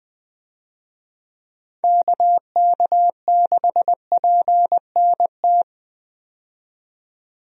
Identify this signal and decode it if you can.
Morse code, heard as a keyed tone, about 20 wpm, 700 Hz. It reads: KK6PNT
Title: USA Call Signs - 6 Characters - 20wpm